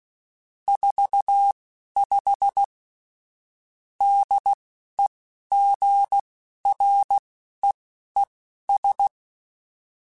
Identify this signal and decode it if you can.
Morse code, heard as a keyed tone, about 16 wpm, spaced as for 12 wpm, 800 Hz. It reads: 45 DEGREES
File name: Weather Sets / Weather Sets-mp3s / Weather 3 17 @10